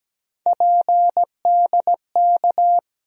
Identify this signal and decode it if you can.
Morse code, heard as a keyed tone, about 17 wpm, 700 Hz. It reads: PDK
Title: USA Call Sign Postfixes - 3 Letters 17wpm